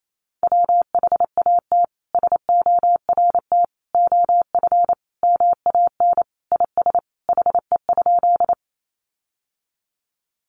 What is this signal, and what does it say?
Morse code, heard as a keyed tone, about 28 wpm, 700 Hz. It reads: WHAT SORT OF MAN IS HE?
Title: Sentences from Top 1300 Words 28wpm